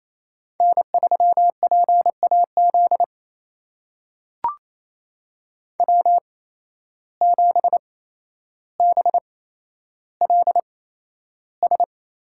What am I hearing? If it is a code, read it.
Morse code, heard as a keyed tone, about 28 wpm, spaced as for 12 wpm, 700 Hz. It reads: N3PAZ W7BLS
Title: USA Call Signs - 5 Characters - 8x Character Spacing - 28wpm